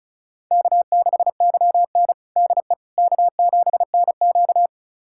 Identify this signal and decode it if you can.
Morse code, heard as a keyed tone, about 36 wpm, 700 Hz. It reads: K6YN DE K7NQ
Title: USA Callsign DE Callsign - 4 Characters - 35wpm